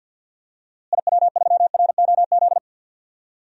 Morse code, heard as a keyed tone, about 52 wpm, 700 Hz. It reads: IW3ROZ
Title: DXCC-100 Call Signs - 6 Characters - Repeated 2x - 50wpm